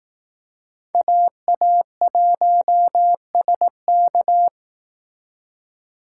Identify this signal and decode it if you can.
Morse code, heard as a keyed tone, about 18 wpm, 700 Hz. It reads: AA1SK
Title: Speed Racing - USA Call Signs 15wpm